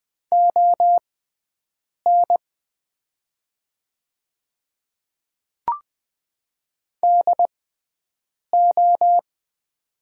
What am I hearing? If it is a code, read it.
Morse code, heard as a keyed tone, about 20 wpm, spaced as for 7 wpm, 700 Hz. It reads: ON DO
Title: 6x Character Spacing - Top 200 Words - 20wpm